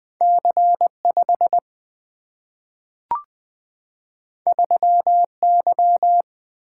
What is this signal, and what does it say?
Morse code, heard as a keyed tone, about 20 wpm, 700 Hz. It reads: C 5 3 Y
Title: Single-Letter Single-Number 20wpm